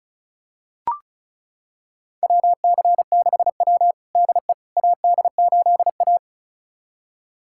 Morse code, heard as a keyed tone, about 34 wpm, 690 Hz. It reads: WC6W DE AD8A
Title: USA Callsign DE Callsign - 4 Characters - 35wpm